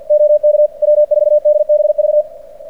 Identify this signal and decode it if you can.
Morse code, heard as a keyed tone, about 50 wpm, 600 Hz. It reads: OK W4DBV
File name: archived / CRSnet